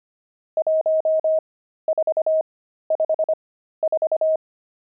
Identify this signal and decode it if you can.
Morse code, heard as a keyed tone, about 25 wpm, spaced as for 13 wpm, 630 Hz. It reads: 1454
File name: pangram-fs-13-cs-25-nums17